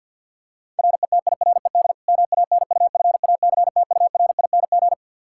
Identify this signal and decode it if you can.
Morse code, heard as a keyed tone, about 48 wpm, 700 Hz. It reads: RETIRED MANUFACTURING
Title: Practice QSO - Exchange 3 - More Info - 50wpm